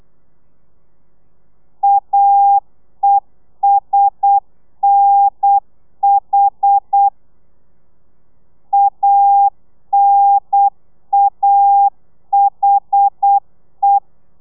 Morse code, heard as a keyed tone, about 8 wpm, 800 Hz. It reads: AESNH ANAHE